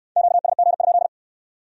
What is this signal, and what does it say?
Morse code, heard as a keyed tone, about 65 wpm, 700 Hz. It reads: QSG?